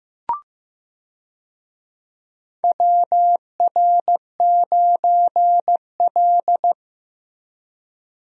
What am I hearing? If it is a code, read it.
Morse code, heard as a keyed tone, about 15 wpm, 700 Hz. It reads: WR9L
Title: USA Call Signs - 4 Characters - Repeated 3x - 15wpm